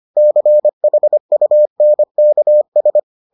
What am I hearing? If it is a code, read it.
Morse code, heard as a keyed tone, about 25 wpm, 600 Hz. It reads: CHUNKS